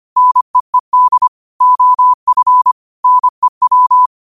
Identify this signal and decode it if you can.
Morse code, heard as a keyed tone, about 25 wpm, 1000 Hz. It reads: NEED OF NEW